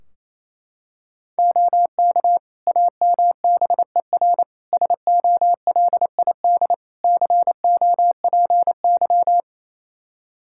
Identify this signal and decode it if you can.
Morse code, heard as a keyed tone, about 28 wpm, 700 Hz. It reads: OK AMBER SOLID COPY
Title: Practice QSO - Exchange 3 - More Info - 28wpm